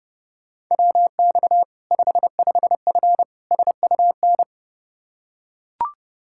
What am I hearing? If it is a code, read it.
Morse code, heard as a keyed tone, about 32 wpm, 700 Hz. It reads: WX 55F SUN
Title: Practice QSO - Exchange 2 - Info - 30wpm